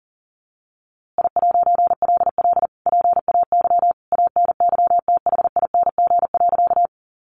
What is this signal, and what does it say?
Morse code, heard as a keyed tone, about 40 wpm, 700 Hz. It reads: I'LL PAY ANYTHING.
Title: A Sound Of Thunder 40wpm